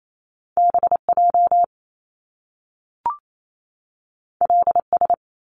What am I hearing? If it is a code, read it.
Morse code, heard as a keyed tone, about 28 wpm, 700 Hz. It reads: BJ LS
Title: Top Two Letter Combinations in Words 28wpm